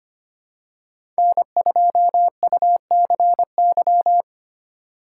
Morse code, heard as a keyed tone, about 25 wpm, 700 Hz. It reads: N2UCY
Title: USA Call Signs - 5 Characters - 3x Character Spacing - 25wpm